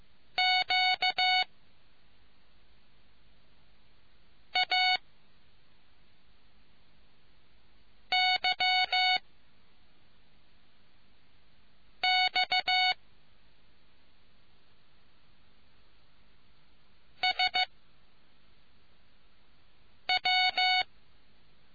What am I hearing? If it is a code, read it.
Morse code, heard as a keyed tone, about 15 wpm, 740 Hz. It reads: Q A Y X S W